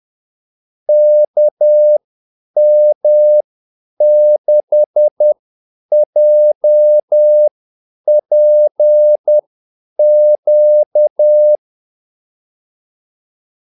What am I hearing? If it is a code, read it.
Morse code, heard as a keyed tone, about 10 wpm, spaced as for 8 wpm, 600 Hz. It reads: KM6JPQ
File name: Callsigns-10-8wpm